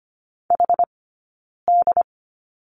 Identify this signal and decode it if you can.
Morse code, heard as a keyed tone, about 25 wpm, 700 Hz. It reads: H D
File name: Lesson 17d - Letter u - Two Characters 25wpm